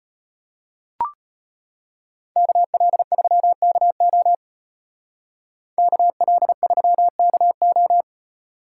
Morse code, heard as a keyed tone, about 34 wpm, 700 Hz. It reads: KL3KO KL3KO